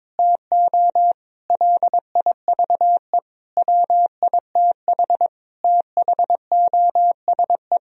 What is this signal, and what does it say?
Morse code, heard as a keyed tone, about 22 wpm, 700 Hz. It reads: TO LIVE WITH THOSE
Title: Sentences from Top 1000 Words - Review Part 1 - 22wpm